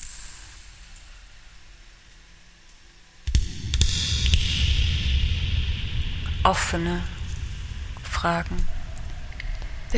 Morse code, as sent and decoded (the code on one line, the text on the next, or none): none